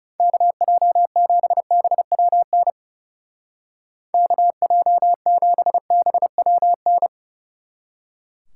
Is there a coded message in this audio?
Yes: Morse code from 0.2 to 7.1 s, heard as a keyed tone, about 40 wpm, 700 Hz. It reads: KJ7BWN KJ7BWN